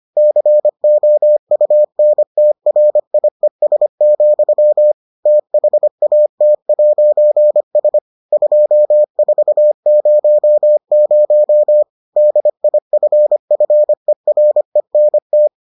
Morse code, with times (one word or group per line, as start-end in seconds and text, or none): 0.2-4.9 COUNTRIES,
5.3-8.0 THAT'S
8.3-11.8 2400
12.2-15.5 DIFFERENT